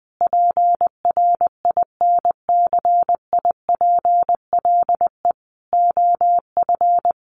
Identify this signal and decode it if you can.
Morse code, heard as a keyed tone, about 20 wpm, 700 Hz. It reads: PRINCIPLE OF
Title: Sentences from Top 1400 Words 20wpm